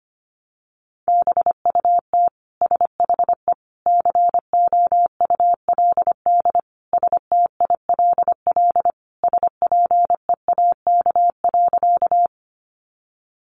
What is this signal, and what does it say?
Morse code, heard as a keyed tone, about 25 wpm, 700 Hz. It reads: BUT SHE COULD STILL SPEAK.
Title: Sentences from Top 1100 Words 25wpm